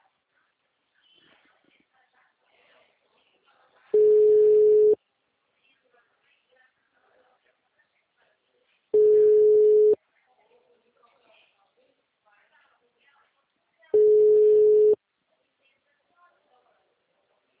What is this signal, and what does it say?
Ringback tone (Europe, 425 Hz), with speech.